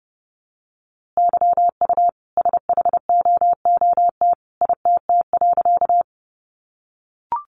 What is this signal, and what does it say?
Morse code, heard as a keyed tone, about 30 wpm, 700 Hz. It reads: YU SHOOT ITT.
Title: A Sound Of Thunder 30wpm